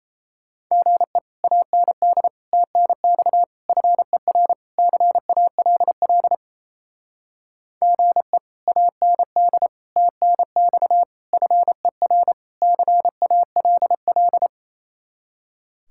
Morse code, heard as a keyed tone, about 35 wpm, 700 Hz. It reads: GE AND TNX FER CALL GE AND TNX FER CALL